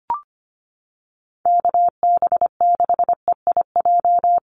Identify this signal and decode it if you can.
Morse code, heard as a keyed tone, about 25 wpm, 700 Hz. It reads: KB6EIJ
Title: USA Call Signs Repeated 2x 25wpm